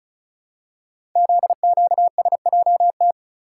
Morse code, heard as a keyed tone, about 36 wpm, 700 Hz. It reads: ZQSJT
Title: Code Groups - 5 Letters - 35wpm